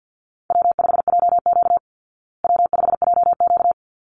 Morse code, heard as a keyed tone, about 49 wpm, 700 Hz. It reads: W5JQ W5JQ
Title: Call Signs Repeated 3x - 50wpm